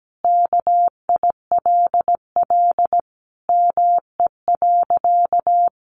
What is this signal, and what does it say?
Morse code, heard as a keyed tone, about 17 wpm, 700 Hz. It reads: KILL ME.
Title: Sentences from Top 1400 Words 17wpm